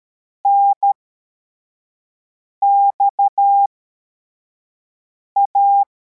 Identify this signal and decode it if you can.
Morse code, heard as a keyed tone, about 13 wpm, spaced as for 4 wpm, 800 Hz. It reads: NXA
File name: formation / cw / Text25